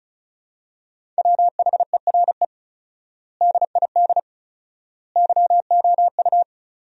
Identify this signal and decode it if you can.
Morse code, heard as a keyed tone, about 34 wpm, 700 Hz. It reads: WHERE DID YOU
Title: 4x Word-Spacing - Sentences from Top 200 Words - 35wpm